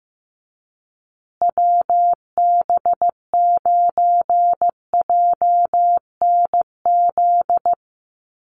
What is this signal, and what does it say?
Morse code, heard as a keyed tone, about 15 wpm, 700 Hz. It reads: WB9JNZ
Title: USA Call Signs Repeated 2x 15wpm